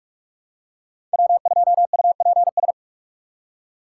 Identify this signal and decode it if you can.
Morse code, heard as a keyed tone, about 46 wpm, 700 Hz. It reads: W2UPS
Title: USA Call Signs - 5 Characters - 2x Character Spacing - 45wpm